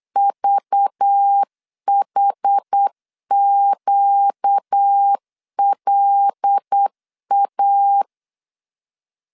Morse code, heard as a keyed tone, about 8 wpm, 800 Hz. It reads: VHQLA
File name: LEZIONI / 6 PKX0 / 6 PKX0 8wpm 3341